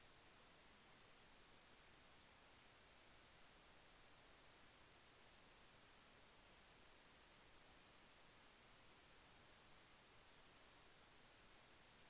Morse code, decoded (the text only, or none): none